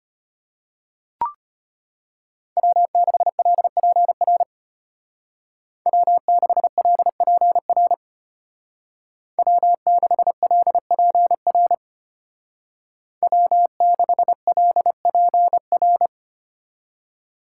Morse code, heard as a keyed tone, about 30 wpm, 700 Hz. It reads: W6LPR W6LPR W6LPR W6LPR
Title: Speed Racing - USA 1x3 Call Signs 25wpm